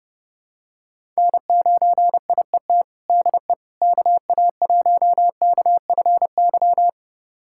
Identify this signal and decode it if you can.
Morse code, heard as a keyed tone, about 30 wpm, 700 Hz. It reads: N9IET DE KA1KFY
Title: USA Callsign DE Callsign 30wpm